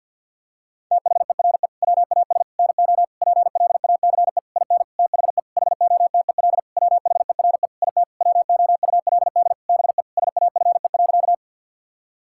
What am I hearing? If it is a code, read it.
Morse code, heard as a keyed tone, about 50 wpm, 700 Hz. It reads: THERE WAS NO PLACE IN THE HOTEL WHERE IT WOULD BE SAFE.
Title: Sentences from Top 1600 Words 50wpm